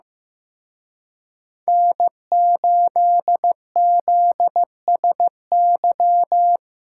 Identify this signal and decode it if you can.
Morse code, heard as a keyed tone, about 18 wpm, 700 Hz. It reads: N8ZSY
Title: Speed Racing - USA 1x3 Call Signs 15wpm